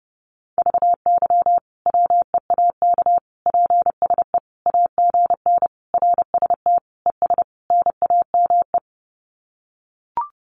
Morse code, heard as a keyed tone, about 30 wpm, 700 Hz. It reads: VY WEAK PSE AGN RST ES NAME
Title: Common QSO Phrases 30wpm